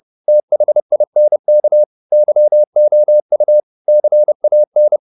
Morse code, heard as a keyed tone, about 30 wpm, 600 Hz. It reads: THINK YOU CAN
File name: the old man and the sea / the old man and the sea 1 30wpm